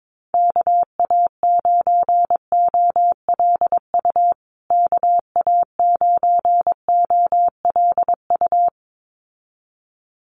Morse code, heard as a keyed tone, about 22 wpm, 700 Hz. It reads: KA9OLU KA9OLU
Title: USA Call Signs Repeated 2x 22wpm